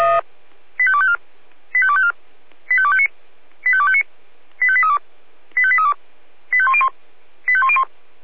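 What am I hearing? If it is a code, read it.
Five-tone selective call (ZVEI), heard as quick runs of five tones, tones of about 70 ms. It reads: 85254 85254 85259 85259 85832 85832 85101 85101
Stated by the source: Five-tone ZVEI selective call 85870 received on an SDR.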